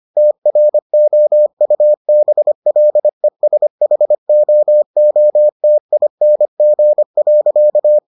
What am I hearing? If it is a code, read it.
Morse code, heard as a keyed tone, about 25 wpm, 600 Hz. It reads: TROUBLESHOOTING.